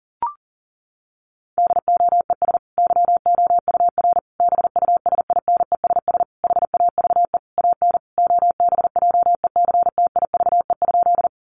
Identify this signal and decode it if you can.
Morse code, heard as a keyed tone, about 40 wpm, 700 Hz. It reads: DOES YOUR BUSINESS HAVE AN OBJECTIVE?
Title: Sentences from Top 1200 Words 40wpm